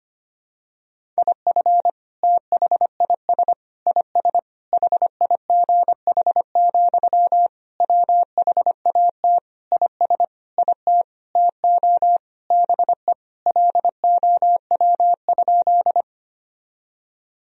Morse code, heard as a keyed tone, about 25 wpm, 700 Hz. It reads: IF THIS IS HIGH, WHAT IS IT TO BE LOW?
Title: Sentences from Top 1000 Words - Review Part 1 - 25wpm